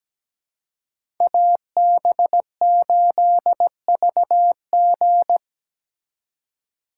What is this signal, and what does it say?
Morse code, heard as a keyed tone, about 20 wpm, 700 Hz. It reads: AB8VG
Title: Speed Racing - USA 2x2 Call Signs 17wpm